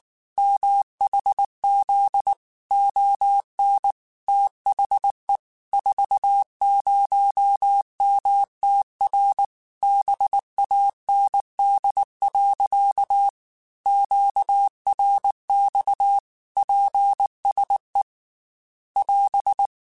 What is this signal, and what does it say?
Morse code, heard as a keyed tone, about 19 wpm, 800 Hz. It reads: MHZ ON THE 40MTR BAND. QRX PSE <AS>
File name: QSOs / QSOs-mp3s / WA7QOX 2@18